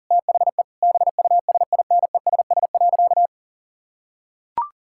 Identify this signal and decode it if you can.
Morse code, heard as a keyed tone, about 40 wpm, 700 Hz. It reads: THE BUSINESS.
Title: Sentences from Top 1000 Words - Review Part 6 - 40wpm